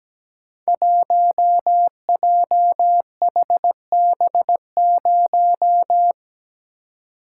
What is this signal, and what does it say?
Morse code, heard as a keyed tone, about 17 wpm, 700 Hz. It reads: 1JHB0
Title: Code Groups - 5 Letter:Numbers - 17wpm